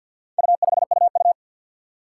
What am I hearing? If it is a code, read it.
Morse code, heard as a keyed tone, about 51 wpm, 700 Hz. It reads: U5UU
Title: Four Challenging Letter-Number Combinations 50wpm